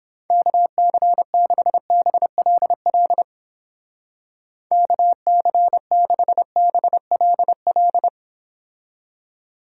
Morse code, heard as a keyed tone, about 31 wpm, 700 Hz. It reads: KC6BLL KC6BLL